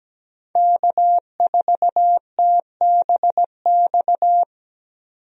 Morse code, heard as a keyed tone, about 17 wpm, 700 Hz. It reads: K4TBX
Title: USA Call Signs - 5 Characters - 17wpm